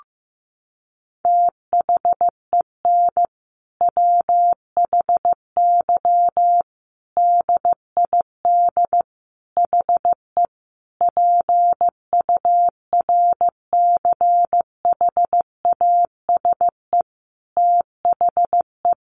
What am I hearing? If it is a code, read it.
Morse code, heard as a keyed tone, about 15 wpm, 700 Hz. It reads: THEN WHY DID HE PURCHASE THE
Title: Sentences from Top 800 Words 15wpm